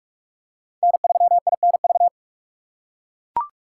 Morse code, heard as a keyed tone, about 41 wpm, 700 Hz. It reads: N3INV